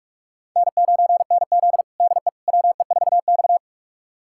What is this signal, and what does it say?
Morse code, heard as a keyed tone, about 45 wpm, 700 Hz. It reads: N9NZ DE WE4X
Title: USA Callsign DE Callsign - 4 Characters - 45wpm